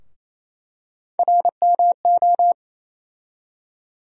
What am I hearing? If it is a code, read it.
Morse code, heard as a keyed tone, about 28 wpm, 700 Hz. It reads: RMO
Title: USA Call Sign Postfixes - 3 Letters 28wpm